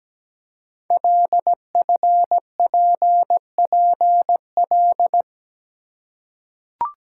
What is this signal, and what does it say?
Morse code, heard as a keyed tone, about 17 wpm, 700 Hz. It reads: LFPPL
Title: Code Groups - 5 Letters - 17wpm